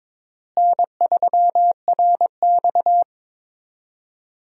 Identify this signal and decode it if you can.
Morse code, heard as a keyed tone, about 22 wpm, 700 Hz. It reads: N3RX
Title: USA Call Signs - 4 Characters - Repeated 3x - 22wpm